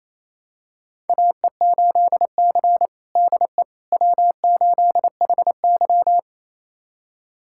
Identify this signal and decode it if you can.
Morse code, heard as a keyed tone, about 28 wpm, 700 Hz. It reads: AE8C DE W8HY
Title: USA Callsign DE Callsign - 4 Characters - 28wpm